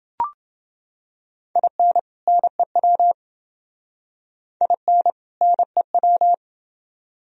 Morse code, heard as a keyed tone, about 30 wpm, 700 Hz. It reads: IN NEW IN NEW